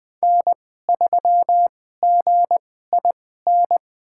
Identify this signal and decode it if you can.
Morse code, heard as a keyed tone, about 20 wpm, spaced as for 14 wpm, 700 Hz. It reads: N3GIN
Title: USA Call Signs - 5 Characters - 2x Character Spacing - 20wpm